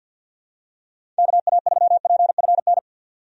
Morse code, heard as a keyed tone, about 50 wpm, 700 Hz. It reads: KA3PFN
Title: USA Call Signs - 6 Characters - 50wpm